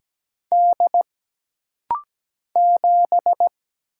Morse code, heard as a keyed tone, about 17 wpm, 700 Hz. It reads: D 7
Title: Single Letter-Number Alt - Rapid-Fire 17wpm